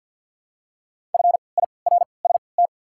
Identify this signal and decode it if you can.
Morse code, heard as a keyed tone, about 53 wpm, spaced as for 29 wpm, 690 Hz. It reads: FIRST
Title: 3x Character Spacing - Top 100 Words - 50wpm